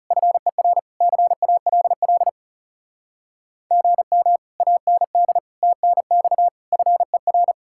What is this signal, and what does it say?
Morse code, heard as a keyed tone, about 37 wpm, 700 Hz. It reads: FER CALL GM AND TNX FER
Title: Speed-Racing Common QSO Phrases Encore - 30wpm